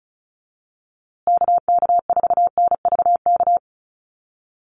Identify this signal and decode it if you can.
Morse code, heard as a keyed tone, about 34 wpm, 700 Hz. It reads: KK4NVK
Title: USA Call Signs Repeated 2x 35wpm